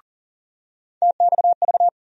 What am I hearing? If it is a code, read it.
Morse code, heard as a keyed tone, about 40 wpm, 700 Hz. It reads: TXV
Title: USA Call Sign Postfixes - 3 Letters 40wpm